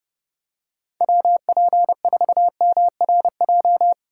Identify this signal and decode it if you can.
Morse code, heard as a keyed tone, about 30 wpm, 700 Hz. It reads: WP4MRJ